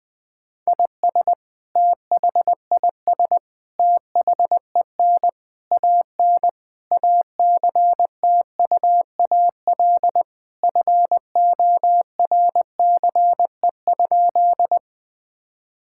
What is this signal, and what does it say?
Morse code, heard as a keyed tone, about 20 wpm, 700 Hz. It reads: IS THIS THEN AN ACTUAL FORCE?
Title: Sentences from Top 1000 Words - Review Part 5 - 20wpm